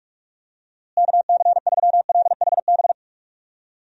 Morse code, heard as a keyed tone, about 46 wpm, 700 Hz. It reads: KC3LHB
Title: USA Call Signs - 6 Characters - Repeated 3x - 45wpm